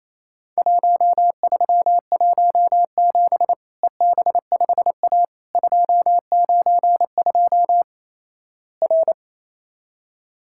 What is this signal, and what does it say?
Morse code, heard as a keyed tone, about 28 wpm, 700 Hz. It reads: 1317 EB5A 292 R
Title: Worked All Europe DX Contest - QTC Exchange - 28wpm